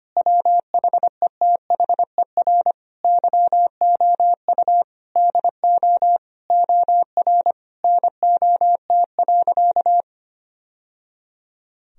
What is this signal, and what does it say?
Morse code, heard as a keyed tone, about 25 wpm, 700 Hz. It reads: WHETHER YOU DO OR NOT.
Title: Sentences from Top 1000 Words - Review Part 3 - 25wpm